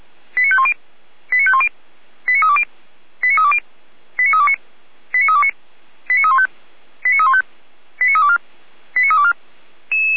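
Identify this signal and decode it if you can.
Five-tone selective call (ZVEI), heard as quick runs of five tones, tones of about 70 ms. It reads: 89510 89510 89320 89320 89329 89329 89315 89315 89324 89324
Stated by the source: ZVEI five-tone selective call 89109 received by SDR.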